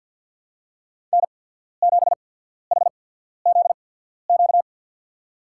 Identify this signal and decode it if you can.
Morse code, heard as a keyed tone, about 53 wpm, spaced as for 14 wpm, 700 Hz. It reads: N7HZQ
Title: USA Call Signs - 5 Characters - 8x Character Spacing - 50wpm